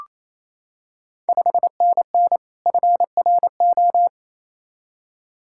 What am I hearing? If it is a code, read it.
Morse code, heard as a keyed tone, about 27 wpm, 690 Hz. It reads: 5NN FRO
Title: Ontario QSO Party Exchange 28wpm